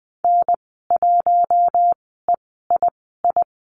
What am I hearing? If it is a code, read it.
Morse code, heard as a keyed tone, about 20 wpm, 700 Hz. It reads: N 1 E I I
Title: 062 Lesson 11g - Letter h - Call Signs 20wpm